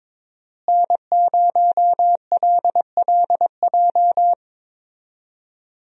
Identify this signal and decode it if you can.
Morse code, heard as a keyed tone, about 22 wpm, 700 Hz. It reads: N0LLJ
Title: USA Call Signs - 5 Characters - Repeated 2x - 22wpm